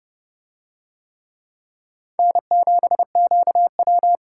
Morse code, heard as a keyed tone, about 30 wpm, 700 Hz. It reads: N7QW